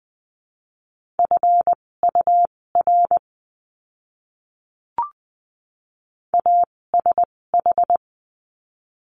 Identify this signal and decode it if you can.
Morse code, heard as a keyed tone, about 20 wpm, spaced as for 16 wpm, 700 Hz. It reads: FUR ASH
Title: 256 Lesson 42a - Words 12-20wpm